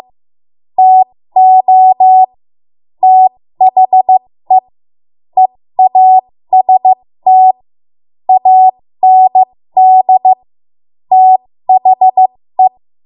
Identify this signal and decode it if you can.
Morse code, heard as a keyed tone, about 15 wpm, spaced as for 13 wpm, 750 Hz. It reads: TO THE EAST AND THE
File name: Morse / 13WPM